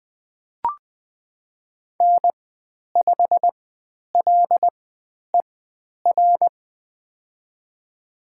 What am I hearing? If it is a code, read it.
Morse code, heard as a keyed tone, about 20 wpm, spaced as for 10 wpm, 700 Hz. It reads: N5LER
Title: 109 Lesson 18g - Letter m - Call Signs 20wpm